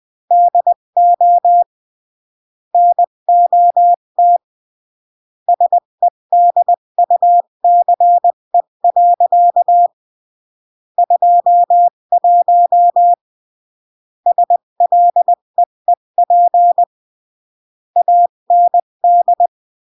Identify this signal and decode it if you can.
Morse code, heard as a keyed tone, about 20 wpm, 700 Hz. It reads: DO NOT SEDUCE. 21 SLEEP AND
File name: CW / jodorowsky / 82 commandments 20-15